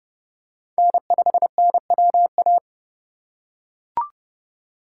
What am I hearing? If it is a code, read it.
Morse code, heard as a keyed tone, about 29 wpm, 700 Hz. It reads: N5NWA